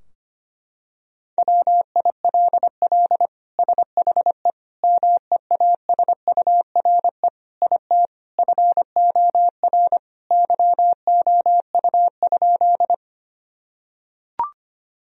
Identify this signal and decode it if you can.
Morse code, heard as a keyed tone, about 25 wpm, 700 Hz. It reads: WILL SHE MEASURE IT FOR YOU?
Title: Sentences from Top 1000 Words - Review Part 2 - 25wpm